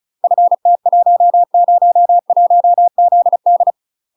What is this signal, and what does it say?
Morse code, heard as a keyed tone, about 35 wpm, 700 Hz. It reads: FT101ZD
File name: UK MASTER QSOs TRAINING / 35WPM / UK MASTER QSOs 0020